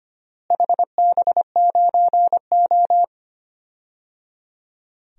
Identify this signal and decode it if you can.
Morse code, heard as a keyed tone, about 25 wpm, 700 Hz. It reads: HB9O
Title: DXCC-100 Call Signs - 4 Characters - Repeated 3x - 25wpm